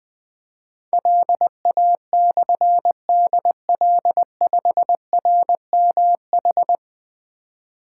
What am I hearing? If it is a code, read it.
Morse code, heard as a keyed tone, about 20 wpm, 700 Hz. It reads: LA/DL5RMH
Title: Difficult Call Signs 20wpm